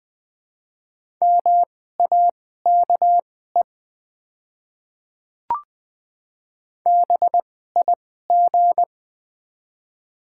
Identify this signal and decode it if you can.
Morse code, heard as a keyed tone, about 20 wpm, spaced as for 14 wpm, 690 Hz. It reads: MAKE BIG
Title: 2x Character Spacing - Top 500 Words - 20wpm